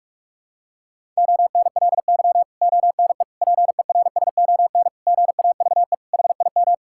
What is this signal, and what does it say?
Morse code, heard as a keyed tone, about 43 wpm, 700 Hz. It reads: ONLY ONE PERSON GAVE HIM
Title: Sentences from Top 1000 Words - Review Part 6 - 45wpm